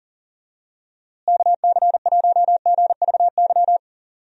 Morse code, heard as a keyed tone, about 40 wpm, 700 Hz. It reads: KC1GVY
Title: USA Call Signs - 6 Characters - Repeated 2x - 40wpm